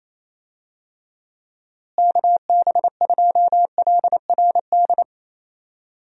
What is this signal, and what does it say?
Morse code, heard as a keyed tone, about 27 wpm, 700 Hz. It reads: KB2LRD
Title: USA Call Signs - 6 Characters - Repeated 3x - 28wpm